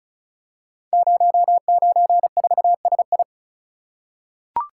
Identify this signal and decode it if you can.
Morse code, heard as a keyed tone, about 36 wpm, 700 Hz. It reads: 094SI
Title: Code Groups - 5 Letter:Numbers - 35wpm